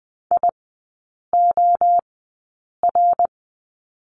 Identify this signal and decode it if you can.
Morse code, heard as a keyed tone, about 20 wpm, spaced as for 8 wpm, 700 Hz. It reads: IOR
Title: 087 Lesson 15e - Number 5 - Three Characters 20wpm